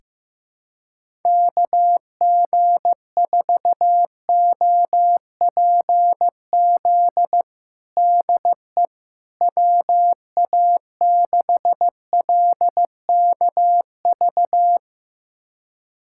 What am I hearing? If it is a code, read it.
Morse code, heard as a keyed tone, about 16 wpm, 700 Hz. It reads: KG4OPZ DE WA6LKV
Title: USA Callsign DE Callsign - 6 Characters - 15wpm